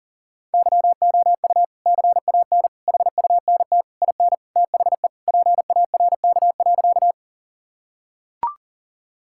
Morse code, heard as a keyed tone, about 37 wpm, 700 Hz. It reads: YOU CAN HUNT IN THE PARK.
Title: Sentences from Top 1600 Words 40wpm